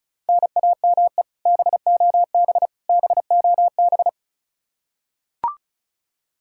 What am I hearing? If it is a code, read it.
Morse code, heard as a keyed tone, about 36 wpm, 700 Hz. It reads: NAME BOB BOB
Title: K1USN SST QSO Practice 35wpm